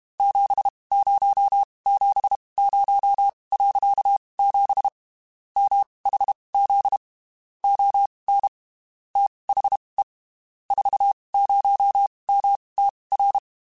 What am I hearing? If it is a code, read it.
Morse code, heard as a keyed tone, about 32 wpm, spaced as for 23 wpm, 800 Hz. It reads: 7070.7 MHZ ON THE 40MTR
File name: (spaced) / QSOs Spaced-mp3s / WA7QOX 2 @30 sp